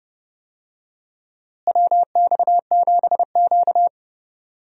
Morse code, heard as a keyed tone, about 30 wpm, 700 Hz. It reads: WX7Q
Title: USA Call Signs - 4 Characters - Repeated 3x - 30wpm